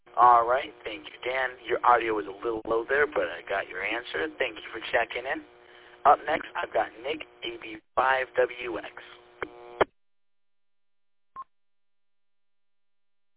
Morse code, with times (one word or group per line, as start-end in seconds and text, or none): none